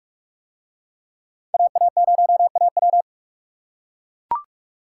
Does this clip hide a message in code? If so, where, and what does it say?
Yes: Morse code from 1.5 to 3.0 s, heard as a keyed tone, about 42 wpm, 700 Hz. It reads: AA0AW